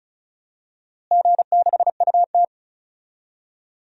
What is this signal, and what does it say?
Morse code, heard as a keyed tone, about 36 wpm, 700 Hz. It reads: G6UT
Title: DXCC-100 Call Signs - 4 Characters - Repeated 3x - 35wpm